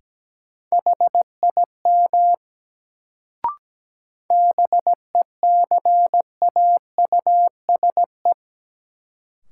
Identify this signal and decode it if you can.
Morse code, heard as a keyed tone, about 17 wpm, 700 Hz. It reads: HIM BECAUSE
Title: Sets of 2 Words - New Word Drill - 17wpm